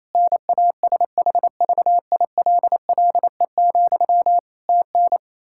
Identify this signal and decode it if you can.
Morse code, heard as a keyed tone, about 28 wpm, 700 Hz. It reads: NASHVILLE, TN
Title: Practice QSO - Exchange 1 - Beginning - 28wpm